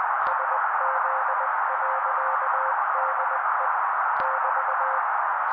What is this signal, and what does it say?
Morse code, heard as a keyed tone, about 20 wpm, 530 Hz. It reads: SZ.DE <BT>
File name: daily / 18wpm noise